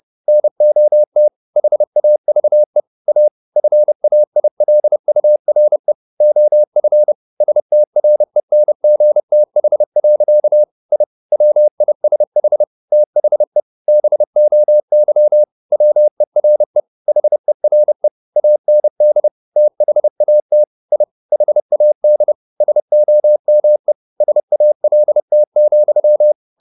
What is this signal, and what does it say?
Morse code, heard as a keyed tone, about 30 wpm, 600 Hz. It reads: NOT HAVE A FAILURE OF STRENGTH. I WISH THE BOY WERE HERE AND THAT I HAD SOME SALT,
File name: the old man and the sea / the old man and the sea 1 30wpm